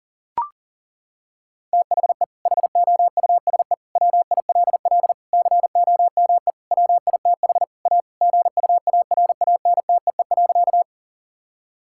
Morse code, heard as a keyed tone, about 40 wpm, 700 Hz. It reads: THE HOUSE WILL COME WITH A GUARANTEE.
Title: Sentences from Top 1600 Words 40wpm